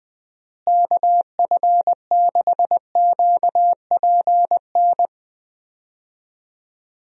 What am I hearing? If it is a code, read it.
Morse code, heard as a keyed tone, about 20 wpm, 700 Hz. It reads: KF6QPN
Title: USA Call Signs - 6 Characters - Repeated 2x - 20wpm